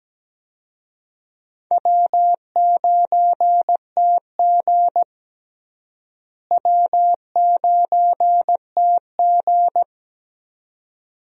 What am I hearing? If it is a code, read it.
Morse code, heard as a keyed tone, about 17 wpm, 700 Hz. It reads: W9TG W9TG
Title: USA Call Signs - 4 Characters - Repeated 2x - 17wpm